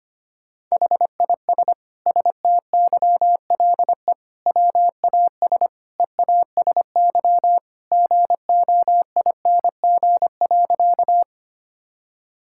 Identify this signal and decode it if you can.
Morse code, heard as a keyed tone, about 25 wpm, 700 Hz. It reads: HIS STYLE WAS EASY GOING.
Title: Sentences from Top 1000 Words - Review Part 5 - 25wpm